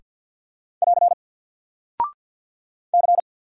Morse code, heard as a keyed tone, about 50 wpm, 700 Hz. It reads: <AR> C
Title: Single Character (Letter:Number:Punctuation:Prosign) 50wpm